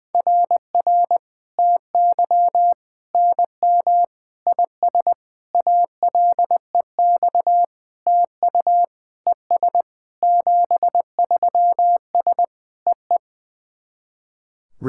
Morse code, heard as a keyed tone, about 20 wpm, 700 Hz. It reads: RR TY NM IS ALEX TU ES 73S EE
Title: SOTA QSO Practice 20wpm